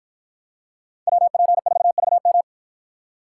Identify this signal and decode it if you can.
Morse code, heard as a keyed tone, about 53 wpm, 700 Hz. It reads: WP4VM